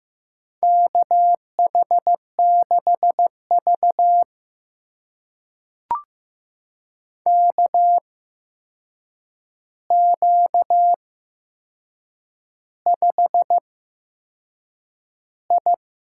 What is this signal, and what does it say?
Morse code, heard as a keyed tone, about 15 wpm, spaced as for 6 wpm, 700 Hz. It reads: KH6V KQ5I
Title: USA Call Signs - 8x Character Spacing - 15wpm